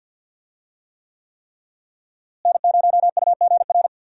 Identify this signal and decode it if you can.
Morse code, heard as a keyed tone, about 51 wpm, 700 Hz. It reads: N0UGR
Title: USA Call Signs - 5 Characters - Repeated 3x - 50wpm